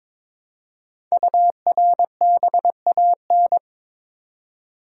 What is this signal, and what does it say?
Morse code, heard as a keyed tone, about 23 wpm, 690 Hz. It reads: URBAN
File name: 5 Letter Words - First 3 Letters Given - 22wpm